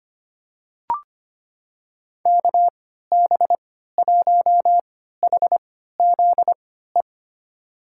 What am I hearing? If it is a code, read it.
Morse code, heard as a keyed tone, about 25 wpm, spaced as for 14 wpm, 700 Hz. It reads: KB1HZE
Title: USA Call Signs - 6 Characters - 3x Character Spacing - 25wpm